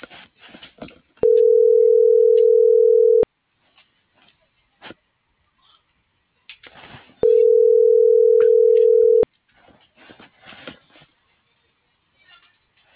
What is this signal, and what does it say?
Ringback tone (North America), 440 + 480 Hz.